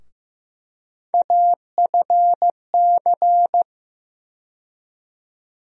Morse code, heard as a keyed tone, about 15 wpm, 700 Hz. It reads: AFC
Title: USA Call Sign Postfixes - 3 Letters 15wpm